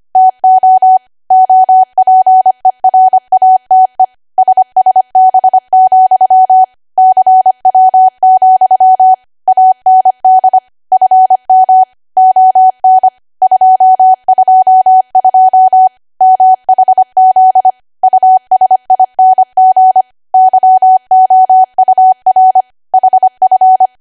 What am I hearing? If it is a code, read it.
Morse code, heard as a keyed tone, about 25 wpm, 750 Hz. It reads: TO OPERATE SSB, CW, AND FM ON 222 MHZ USING YOUR HF
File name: Archive / 25 WPM / 25WPM